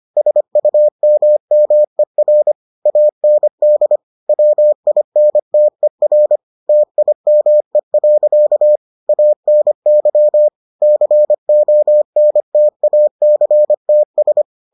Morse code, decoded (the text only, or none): SUMMER AND WINTER TIME. ANY CONTACTS